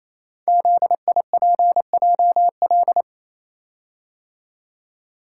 Morse code, heard as a keyed tone, about 27 wpm, 700 Hz. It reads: ZIPJL
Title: Code Groups - 5 Letter:Numbers - 28wpm